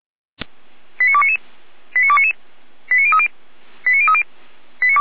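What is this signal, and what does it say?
Five-tone selective call (ZVEI), heard as quick runs of five tones, tones of about 80 ms. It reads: 89299 89299 89030 89030
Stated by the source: ZVEI five-tone selective call 89075 received by SDR.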